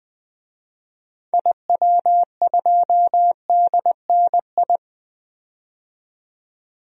Morse code, heard as a keyed tone, about 21 wpm, 700 Hz. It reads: IW2DNI
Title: DXCC-100 Call Signs - 6 Characters - 20wpm